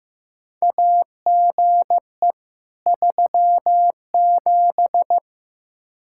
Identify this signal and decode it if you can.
Morse code, heard as a keyed tone, about 15 wpm, 700 Hz. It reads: AGE 37
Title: K1USN SST QSO Practice 15wpm